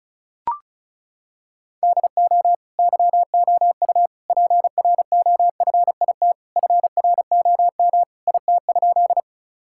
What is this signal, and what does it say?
Morse code, heard as a keyed tone, about 36 wpm, 700 Hz. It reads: DO YOU PROFIT FROM IT?
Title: Sentences from Top 1000 Words - Review Part 5 - 35wpm